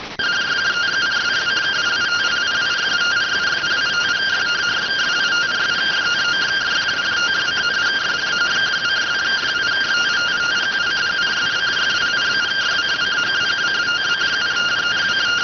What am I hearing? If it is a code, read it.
Two-tone teleprinter signal, about 170 Hz shift.